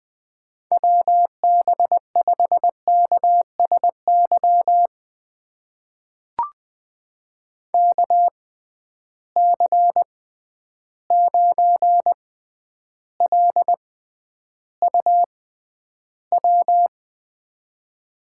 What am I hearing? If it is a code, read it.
Morse code, heard as a keyed tone, about 20 wpm, spaced as for 10 wpm, 700 Hz. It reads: WB5KSY KC9LUW
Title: USA Call Signs - 6 Characters - 6x Character Spacing - 20wpm